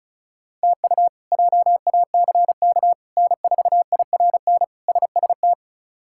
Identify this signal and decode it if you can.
Morse code, heard as a keyed tone, about 34 wpm, 700 Hz. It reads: TU JACK N4IRN SST
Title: K1USN SST QSO Practice - 35wpm